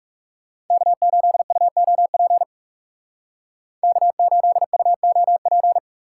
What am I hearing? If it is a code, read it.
Morse code, heard as a keyed tone, about 35 wpm, 700 Hz. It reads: K8UOP K8UOP
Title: Speed Racing - USA 1x3 Call Signs 30wpm